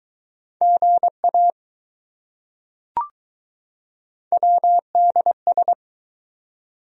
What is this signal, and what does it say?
Morse code, heard as a keyed tone, about 20 wpm, 700 Hz. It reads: GA WDS